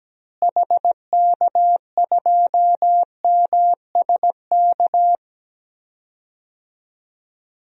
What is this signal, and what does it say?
Morse code, heard as a keyed tone, about 17 wpm, 700 Hz. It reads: HK2MSK
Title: DXCC-100 Call Signs - 6 Characters - Repeated 3x - 17wpm